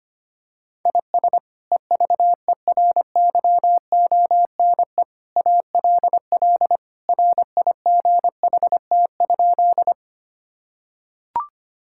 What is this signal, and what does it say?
Morse code, heard as a keyed tone, about 25 wpm, 700 Hz. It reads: IS EVERYONE ALL RIGHT?
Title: Sentences from Top 1000 Words - Review Part 5 - 25wpm